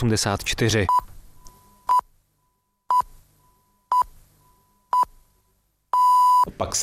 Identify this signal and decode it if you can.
Radio time signal (time pips), with speech. It is heard from 0.9 to 6.4 s.